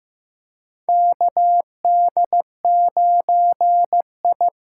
Morse code, heard as a keyed tone, about 15 wpm, 700 Hz. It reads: KD9I
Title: USA Call Signs - 6 Characters - 6x Character Spacing - 15wpm